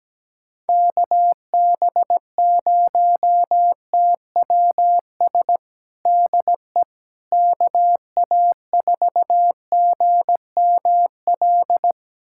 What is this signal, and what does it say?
Morse code, heard as a keyed tone, about 17 wpm, 700 Hz. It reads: KB0TWS DE KA4GML
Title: USA Callsign DE Callsign - 6 Characters - 17wpm